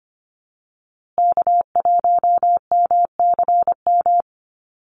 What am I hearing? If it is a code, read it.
Morse code, heard as a keyed tone, about 25 wpm, 700 Hz. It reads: K1MCM
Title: USA Call Signs Repeated 3x 25wpm